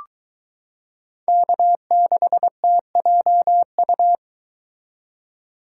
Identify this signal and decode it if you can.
Morse code, heard as a keyed tone, about 22 wpm, 700 Hz. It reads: K6TJU